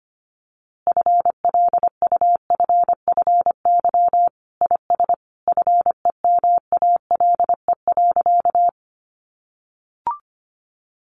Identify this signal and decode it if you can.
Morse code, heard as a keyed tone, about 25 wpm, 700 Hz. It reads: FLUFFY IS FEMALE.
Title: Easy Phrases 25wpm